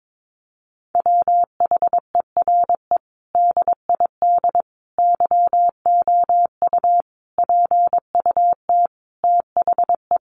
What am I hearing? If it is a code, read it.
Morse code, heard as a keyed tone, about 22 wpm, 700 Hz. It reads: WHERE DID YOU PUT THE
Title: Sentences from Top 1200 Words 22wpm